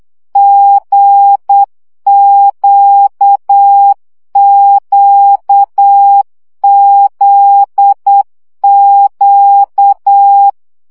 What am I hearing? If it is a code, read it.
Morse code, heard as a keyed tone, about 8 wpm, 800 Hz. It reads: GQQZQ
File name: CW gyakorlas g z q 7 baud